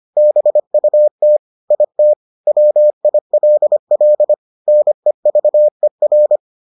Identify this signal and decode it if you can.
Morse code, heard as a keyed tone, about 25 wpm, 600 Hz. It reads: BUT IT WILL NEVER